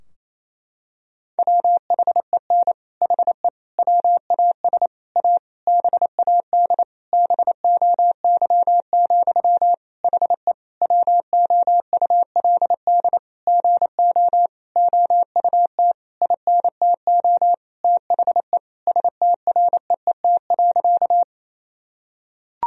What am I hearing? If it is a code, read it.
Morse code, heard as a keyed tone, about 28 wpm, 700 Hz. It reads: WHEN HE WAS A BAD BOY, HE WOULD GO OUT INTO THE STREET.
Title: Sentences from Top 1000 Words - Review Part 3 - 28wpm